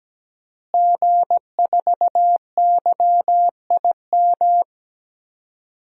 Morse code, heard as a keyed tone, about 17 wpm, 700 Hz. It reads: G4YIM
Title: DXCC-100 Call Signs - 5 Characters - Repeated 3x - 17wpm